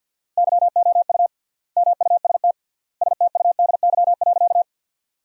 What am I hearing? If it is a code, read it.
Morse code, heard as a keyed tone, about 50 wpm, 700 Hz. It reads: YOU MUST STUDY.
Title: 3x Word-Spacing - Sentences from Top 300 Words - 50wpm